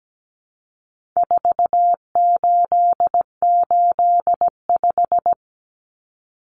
Morse code, heard as a keyed tone, about 17 wpm, 700 Hz. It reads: 4885
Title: Common Cut Numbers - 4-Digit 17wpm